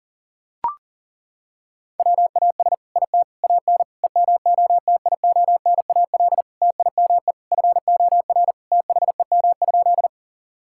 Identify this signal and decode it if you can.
Morse code, heard as a keyed tone, about 40 wpm, 700 Hz. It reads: WAS IT AN EMOTIONAL TIME FOR THEM?
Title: Sentences from Top 1000 Words - Review Part 5 - 40wpm